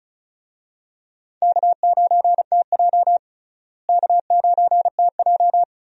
Morse code, heard as a keyed tone, about 34 wpm, 700 Hz. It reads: K9TJ K9TJ
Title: USA Call Signs - 4 Characters - Repeated 2x - 35wpm